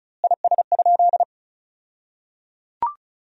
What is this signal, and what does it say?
Morse code, heard as a keyed tone, about 36 wpm, 700 Hz. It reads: IS?
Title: 2x Word-Spacing - Sentences from Top 100 Words - 35wpm